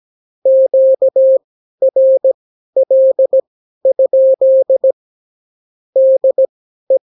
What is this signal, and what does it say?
Morse code, heard as a keyed tone, about 17 wpm, spaced as for 12 wpm, 530 Hz. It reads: QRL? DE
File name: CW QSO libroCW4 18-8ppm